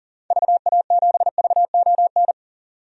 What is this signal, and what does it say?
Morse code, heard as a keyed tone, about 40 wpm, 700 Hz. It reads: VA7VON
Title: DXCC-100 Call Signs - 6 Characters - 40wpm